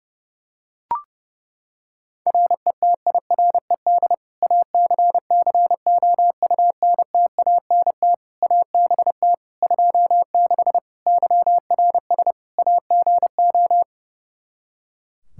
Morse code, heard as a keyed tone, about 30 wpm, 700 Hz. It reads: RETIRED ACCOUNTANT ABT 26 YRS AGO
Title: Practice QSO - Exchange 3 - More Info - 30wpm